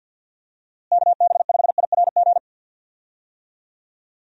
Morse code, heard as a keyed tone, about 50 wpm, 700 Hz. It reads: KB5IRG
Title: USA Call Signs - 6 Characters - Repeated 3x - 50wpm